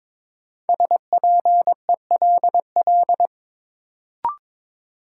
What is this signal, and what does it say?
Morse code, heard as a keyed tone, about 21 wpm, 700 Hz. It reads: SPELL
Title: Sentences from Top 1000 Words - Review Part 1 - New Word Drill - 22wpm